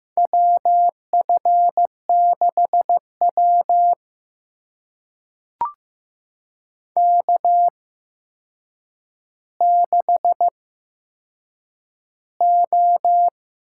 Morse code, heard as a keyed tone, about 15 wpm, spaced as for 7 wpm, 700 Hz. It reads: WF6W K6O
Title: USA Call Signs - 4 Characters - 8x Character Spacing - 15wpm